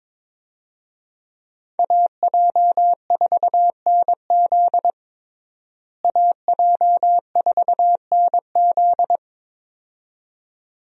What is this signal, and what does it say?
Morse code, heard as a keyed tone, about 22 wpm, 700 Hz. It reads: AJ4NZ AJ4NZ
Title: USA Call Signs - 5 Characters - Repeated 2x - 22wpm